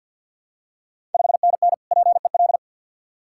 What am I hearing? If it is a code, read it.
Morse code, heard as a keyed tone, about 53 wpm, 690 Hz. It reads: 5NN PEL
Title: Ontario QSO Party Exchange 50wpm